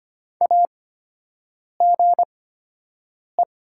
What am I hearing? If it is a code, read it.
Morse code, heard as a keyed tone, about 25 wpm, spaced as for 7 wpm, 700 Hz. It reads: AGE
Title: 8x Character Spacing - Top 500 Words - 25wpm